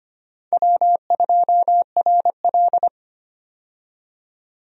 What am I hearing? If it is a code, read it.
Morse code, heard as a keyed tone, about 25 wpm, 700 Hz. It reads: W2RL
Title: USA Call Signs - 4 Characters - 25wpm